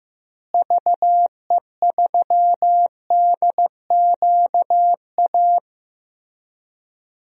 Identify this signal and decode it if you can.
Morse code, heard as a keyed tone, about 15 wpm, 700 Hz. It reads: VE3DQA
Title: DXCC-100 Call Signs - 6 Characters - Repeated 3x - 15wpm